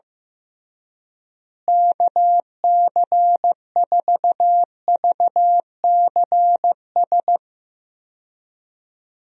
Morse code, heard as a keyed tone, about 16 wpm, 700 Hz. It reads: KC4VCS
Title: USA Call Signs - 6 Characters - Repeated 2x - 15wpm